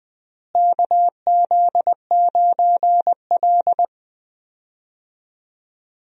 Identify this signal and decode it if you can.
Morse code, heard as a keyed tone, about 20 wpm, 700 Hz. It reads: KZ9L
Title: USA Call Signs - 4 Characters - Repeated 3x - 20wpm